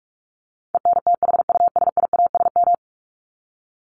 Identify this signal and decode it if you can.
Morse code, heard as a keyed tone, about 44 wpm, 700 Hz. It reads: ENTHUSIASM
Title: Top 1751-2000 Words 45wpm